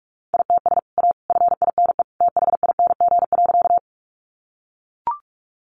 Morse code, heard as a keyed tone, about 44 wpm, 700 Hz. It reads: ITS A FINE THING.
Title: Sentences from Top 600 Words 45wpm